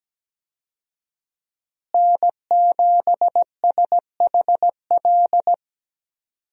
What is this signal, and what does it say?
Morse code, heard as a keyed tone, about 17 wpm, 700 Hz. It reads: N7SHL
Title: USA Call Signs - 5 Characters - Repeated 2x - 17wpm